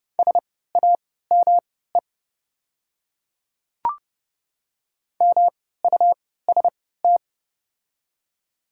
Morse code, heard as a keyed tone, about 30 wpm, spaced as for 17 wpm, 700 Hz. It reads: SAME MUST
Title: 3x Character Spacing - Top 300 Words - 30wpm